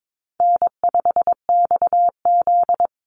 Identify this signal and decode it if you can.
Morse code, heard as a keyed tone, about 22 wpm, 700 Hz. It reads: N5XZ
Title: Easy Call Signs Repeated 3x - 22wpm